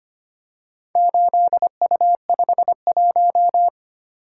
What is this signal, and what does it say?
Morse code, heard as a keyed tone, about 25 wpm, 700 Hz. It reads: 8U51
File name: practice / 25wpm / Four Letter-Number Combinations - 25wpm